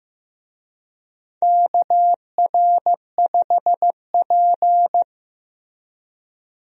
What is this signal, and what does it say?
Morse code, heard as a keyed tone, about 15 wpm, 700 Hz. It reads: KR5P
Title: USA Call Signs - 4 Characters - Repeated 3x - 15wpm